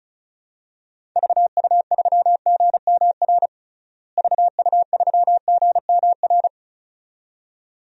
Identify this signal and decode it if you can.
Morse code, heard as a keyed tone, about 36 wpm, 700 Hz. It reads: VU3GMR VU3GMR
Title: DXCC-100 Call Signs - 6 Characters - Repeated 2x - 35wpm